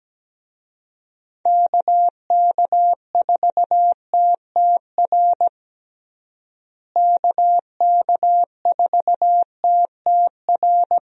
Morse code, heard as a keyed tone, about 17 wpm, 700 Hz. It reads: KK4TTR KK4TTR
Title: USA Call Signs - 6 Characters - Repeated 2x - 17wpm